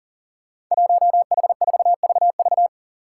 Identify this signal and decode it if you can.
Morse code, heard as a keyed tone, about 40 wpm, 700 Hz. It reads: 1H4VV
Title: Five Challenging Letter-Number Combinations 40wpm